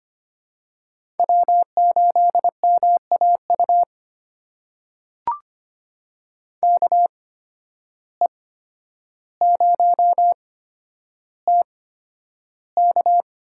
Morse code, heard as a keyed tone, about 25 wpm, spaced as for 11 wpm, 700 Hz. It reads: W8MAU KE0TK
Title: USA Call Signs - 5 Characters - 8x Character Spacing - 25wpm